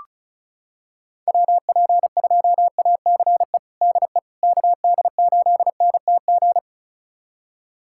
Morse code, heard as a keyed tone, about 36 wpm, 690 Hz. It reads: WP2ACE DE KD8NTG
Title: USA Callsign DE Callsign - 6 Characters - 35wpm